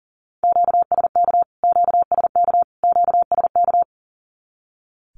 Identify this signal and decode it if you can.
Morse code, heard as a keyed tone, about 40 wpm, 700 Hz. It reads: QSK QSK QSK
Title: Q-Codes Repeated 3x 40wpm